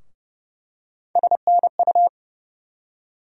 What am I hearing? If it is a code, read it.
Morse code, heard as a keyed tone, about 30 wpm, 700 Hz. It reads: SNU
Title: USA Call Sign Postfixes - 3 Letters 30wpm